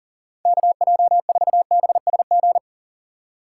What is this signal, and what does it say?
Morse code, heard as a keyed tone, about 39 wpm, 700 Hz. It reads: KJ4BSG
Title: USA Call Signs - 6 Characters - 4x Character Spacing - 40wpm